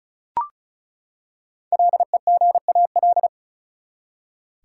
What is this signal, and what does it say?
Morse code, heard as a keyed tone, about 36 wpm, 700 Hz. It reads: LEGAL
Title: Five Letter Words - 35wpm